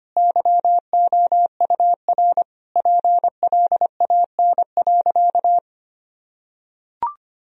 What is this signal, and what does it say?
Morse code, heard as a keyed tone, about 25 wpm, 700 Hz. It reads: YOUR PLAN.
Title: Sentences from Top 1000 Words - Review Part 2 - 25wpm